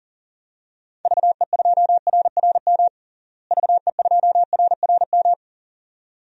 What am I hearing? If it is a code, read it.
Morse code, heard as a keyed tone, about 37 wpm, 700 Hz. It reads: VE2RRM VE2RRM
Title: DXCC-100 Call Signs - 6 Characters - Repeated 2x - 40wpm